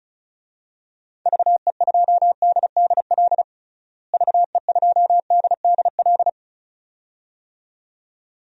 Morse code, heard as a keyed tone, about 34 wpm, 700 Hz. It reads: VE2DDL VE2DDL
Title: DXCC-100 Call Signs - 6 Characters - Repeated 2x - 35wpm